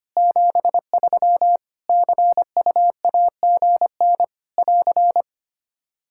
Morse code, heard as a keyed tone, about 25 wpm, 700 Hz. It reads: 73 CUAGN <AR>
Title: Practice QSO - Exchange 4 - Ending - 25wpm